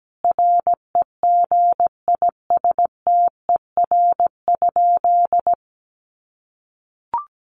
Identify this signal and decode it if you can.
Morse code, heard as a keyed tone, about 17 wpm, 700 Hz. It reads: REGISTER?
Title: Sentences from Top 1200 Words 17wpm